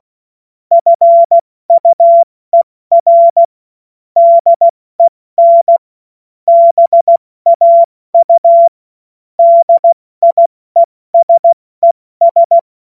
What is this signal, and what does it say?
Morse code, heard as a keyed tone, about 16 wpm, 680 Hz. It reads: FUER DEN BAU DIESES
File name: Morsekurs / Witze / Witz 30. Dezember 2025 70 F80